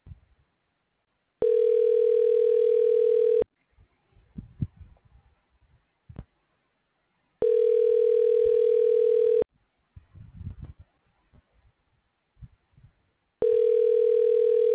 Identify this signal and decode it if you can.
Ringback tone (North America), 440 + 480 Hz.